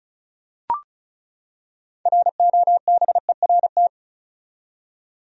Morse code, heard as a keyed tone, about 36 wpm, 700 Hz. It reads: ROBERT
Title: USA Common Names - 35wpm